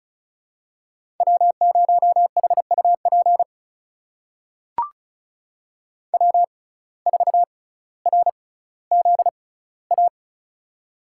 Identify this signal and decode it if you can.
Morse code, heard as a keyed tone, about 36 wpm, spaced as for 18 wpm, 700 Hz. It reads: W0HUP W4RZA
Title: USA Call Signs - 5 Characters - 6x Character Spacing - 35wpm